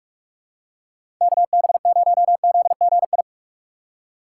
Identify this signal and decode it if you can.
Morse code, heard as a keyed tone, about 49 wpm, 700 Hz. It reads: KB0ZGI